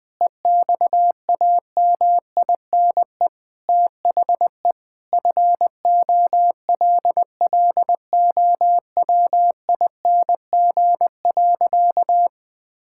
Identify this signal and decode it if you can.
Morse code, heard as a keyed tone, about 21 wpm, 700 Hz. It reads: EXAMINE THE FOLLOWING.
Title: Sentences with 9-Word Phrases - 20wpm